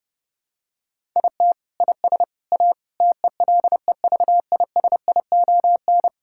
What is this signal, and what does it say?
Morse code, heard as a keyed tone, about 30 wpm, 700 Hz. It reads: IT IS A TELEVISION
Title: Sentences from Top 1000 Words - Review Part 5 - 30wpm